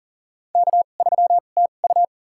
Morse code, heard as a keyed tone, about 40 wpm, spaced as for 29 wpm, 700 Hz. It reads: K3TU
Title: USA Call Signs - 4 Characters - 2x Character Spacing - 40wpm